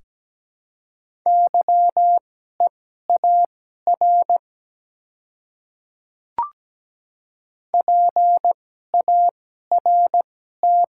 Morse code, heard as a keyed tone, about 17 wpm, spaced as for 12 wpm, 700 Hz. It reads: YEAR PART
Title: 2x Character Spacing - Top 200 Words - 17wpm